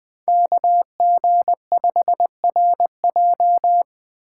Morse code, heard as a keyed tone, about 20 wpm, 700 Hz. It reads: KG5RJ
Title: Speed Racing - USA 2x2 Call Signs 17wpm